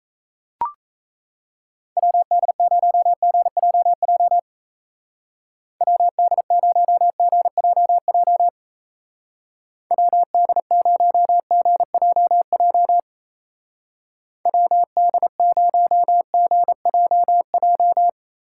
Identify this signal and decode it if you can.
Morse code, heard as a keyed tone, about 34 wpm, 700 Hz. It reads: WD0GJJ WD0GJJ WD0GJJ WD0GJJ
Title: Speed Racing - USA 2x3 Call Signs 28wpm